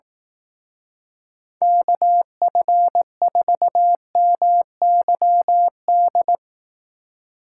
Morse code, heard as a keyed tone, about 18 wpm, 700 Hz. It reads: KF4MYD
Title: Speed Racing - USA 2x3 Call Signs 15wpm